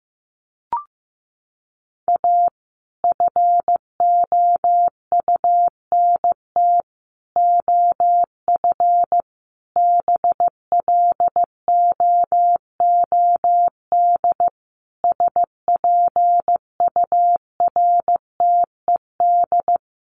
Morse code, heard as a keyed tone, about 15 wpm, 700 Hz. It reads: A FOUNT OF BLOOD SPURTED
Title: A Sound Of Thunder 15wpm